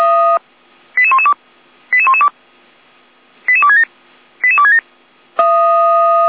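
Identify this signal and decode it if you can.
Five-tone selective call (ZVEI), heard as quick runs of five tones, tones of about 70 ms. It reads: 88112 88112 88267 88267
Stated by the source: ZVEI five-tone selective call received by SDR.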